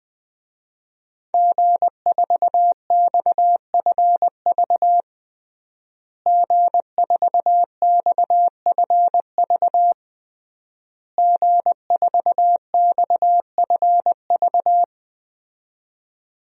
Morse code, heard as a keyed tone, about 20 wpm, 700 Hz. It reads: G4XFV G4XFV G4XFV
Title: DXCC-100 Call Signs - 5 Characters - Repeated 3x - 20wpm